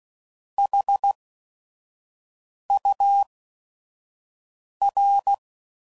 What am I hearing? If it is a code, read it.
Morse code, heard as a keyed tone, about 16 wpm, spaced as for 5 wpm, 800 Hz. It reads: HUR
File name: CW Code Groups @5WPM / Technical Subjects / AND Gate